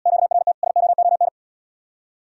Morse code, heard as a keyed tone, about 74 wpm, 700 Hz. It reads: QRT SOON